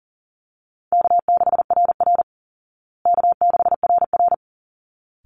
Morse code, heard as a keyed tone, about 40 wpm, 700 Hz. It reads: K6RR K6RR
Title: Call Signs Repeated 3x - 40wpm